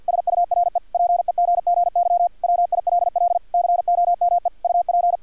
Morse code, heard as a keyed tone, about 50 wpm, 700 Hz, with radio noise. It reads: SURE PEGGY WILL COME UP
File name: archived / CRSnet